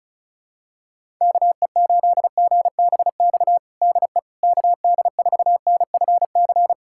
Morse code, heard as a keyed tone, about 36 wpm, 700 Hz. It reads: KE8GBX DE KD4NFC
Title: USA Callsign DE Callsign - 6 Characters - 35wpm